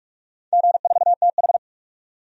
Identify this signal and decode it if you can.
Morse code, heard as a keyed tone, about 42 wpm, 700 Hz. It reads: G4TH
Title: DXCC-100 Call Signs - 4 Characters - 45wpm